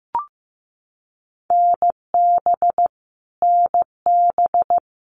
Morse code, heard as a keyed tone, about 15 wpm, 700 Hz. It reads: NB NB
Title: Canadian Provinces and Territories - Repeated 3x - 15wpm